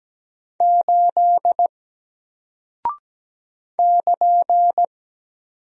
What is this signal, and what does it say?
Morse code, heard as a keyed tone, about 17 wpm, 700 Hz. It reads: 8 <KN>
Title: Single Character (Letter:Number:Punctuation:Prosign) Alt 17wpm